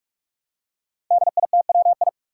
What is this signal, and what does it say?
Morse code, heard as a keyed tone, about 46 wpm, 700 Hz. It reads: DITWI
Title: Five Challenging Letter-Number Combinations 45wpm